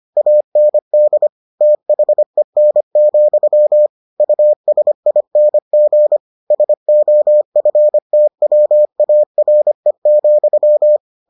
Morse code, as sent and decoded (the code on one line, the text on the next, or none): .- -. -.. / - .... -. --..-- / ..- ... .. -. --. / ... --- ..-. - .-- .- .-. --..--
AND THEN, USING SOFTWARE,